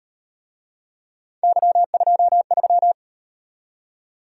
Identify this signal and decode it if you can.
Morse code, heard as a keyed tone, about 38 wpm, 700 Hz. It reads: Y23